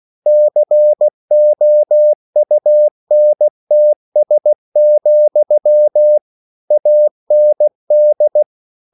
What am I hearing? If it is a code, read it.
Morse code, heard as a keyed tone, about 16 wpm, 600 Hz. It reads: COUNTS, AND
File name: CW / Practice / EnglishQuotations0116